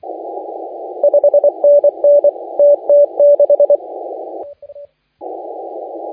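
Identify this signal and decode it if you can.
Morse code, heard as a keyed tone, about 24 wpm, 580 Hz. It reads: 5NN TT6
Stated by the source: Morse contest audio recorded by LA4C during the SAC CW 2018 contest.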